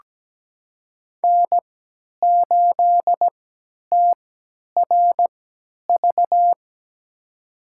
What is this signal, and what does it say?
Morse code, heard as a keyed tone, about 17 wpm, spaced as for 10 wpm, 700 Hz. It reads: N8TRV
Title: USA Call Signs - 5 Characters - 3x Character Spacing - 17wpm